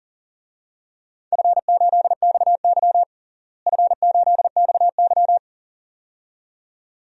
Morse code, heard as a keyed tone, about 37 wpm, 700 Hz. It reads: F8XY F8XY
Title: DXCC-100 Call Signs - 4 Characters - Repeated 2x - 40wpm